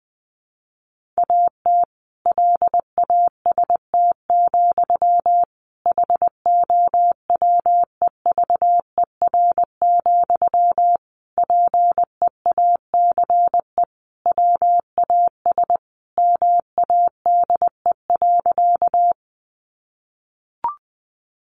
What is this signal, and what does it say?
Morse code, heard as a keyed tone, about 20 wpm, 700 Hz. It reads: AT LAST, HOWEVER, PEACE WAS MADE.
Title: Sentences from Top 1500 Words 20wpm